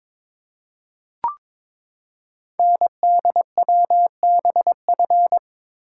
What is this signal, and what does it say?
Morse code, heard as a keyed tone, about 22 wpm, 700 Hz. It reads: NDWBF
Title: Code Groups - 5 Letters - 22wpm